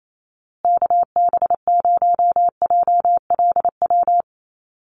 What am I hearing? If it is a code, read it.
Morse code, heard as a keyed tone, about 28 wpm, 700 Hz. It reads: KB0JLW
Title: USA Call Signs Repeated 2x 28wpm